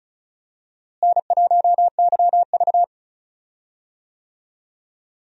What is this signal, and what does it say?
Morse code, heard as a keyed tone, about 36 wpm, 700 Hz. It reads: N1YV
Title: USA Call Signs - 4 Characters - Repeated 2x - 35wpm